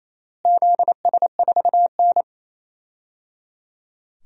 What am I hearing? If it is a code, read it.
Morse code, heard as a keyed tone, about 28 wpm, 700 Hz. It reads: ZS4N
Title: DXCC-100 Call Signs - 4 Characters - Repeated 3x - 28wpm